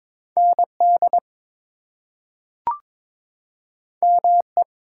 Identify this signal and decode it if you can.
Morse code, heard as a keyed tone, about 22 wpm, 700 Hz. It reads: ND ME
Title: US States and Canadian Provinces 22wpm